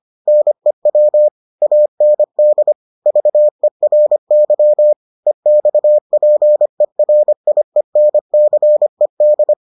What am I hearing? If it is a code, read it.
Morse code, heard as a keyed tone, about 25 wpm, 600 Hz. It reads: NEW AND VERY EXPERIENCED